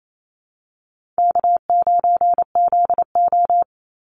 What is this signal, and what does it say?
Morse code, heard as a keyed tone, about 28 wpm, 700 Hz. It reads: K9ZO
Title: Call Signs - 28wpm